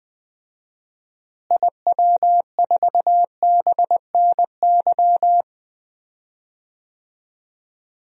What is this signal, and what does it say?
Morse code, heard as a keyed tone, about 20 wpm, 700 Hz. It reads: IW4BNY
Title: DXCC-100 Call Signs - 6 Characters - Repeated 3x - 20wpm